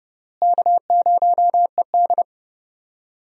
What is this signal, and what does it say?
Morse code, heard as a keyed tone, about 30 wpm, 700 Hz. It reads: K0ED
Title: USA Call Signs - 4 Characters - Repeated 3x - 30wpm